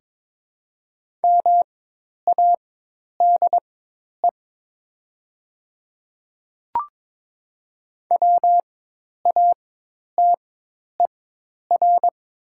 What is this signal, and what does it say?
Morse code, heard as a keyed tone, about 22 wpm, spaced as for 10 wpm, 700 Hz. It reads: MADE WATER
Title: 4x Character Spacing - Top 200 Words - 22wpm